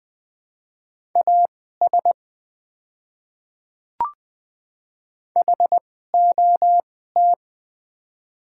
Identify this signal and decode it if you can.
Morse code, heard as a keyed tone, about 20 wpm, spaced as for 15 wpm, 690 Hz. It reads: AS HOT
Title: 2x Character Spacing - Top 200 Words - 20wpm